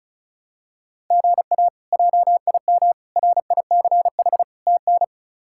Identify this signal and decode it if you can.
Morse code, heard as a keyed tone, about 34 wpm, 700 Hz. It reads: GA JIM RICH TN
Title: K1USN SST QSO Practice - 35wpm